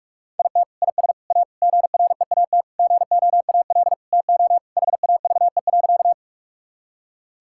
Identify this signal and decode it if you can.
Morse code, heard as a keyed tone, about 41 wpm, 700 Hz. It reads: IT IS A GREAT GOAL TO HAVE.
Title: Sentences from Top 1000 Words - Review Part 5 - 45wpm